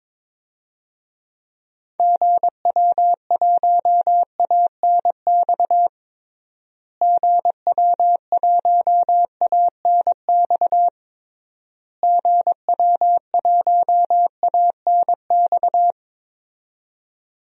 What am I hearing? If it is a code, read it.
Morse code, heard as a keyed tone, about 22 wpm, 700 Hz. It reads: GW1ANX GW1ANX GW1ANX
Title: DXCC-100 Call Signs - 6 Characters - Repeated 3x - 22wpm